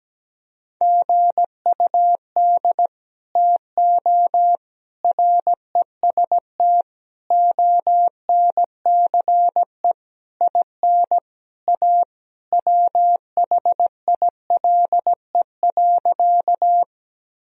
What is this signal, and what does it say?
Morse code, heard as a keyed tone, about 17 wpm, 700 Hz. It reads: GUD TO REST ONCE IN A WHILE.
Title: Sentences from Top 500 Words and Abbreviations 17wpm